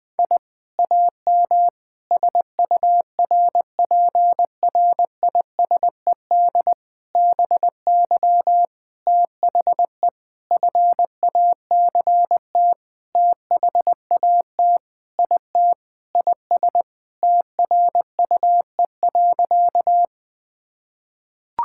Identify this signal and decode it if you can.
Morse code, heard as a keyed tone, about 20 wpm, 700 Hz. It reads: I AM SURPRISED BY THE FACT THAT IT IS TRUE.
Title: Sentences with 4-Word Phrases - Spoken Beginning - 20wpm